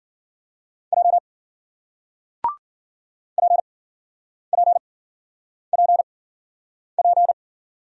Morse code, heard as a keyed tone, about 56 wpm, spaced as for 9 wpm, 700 Hz. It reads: J PPPP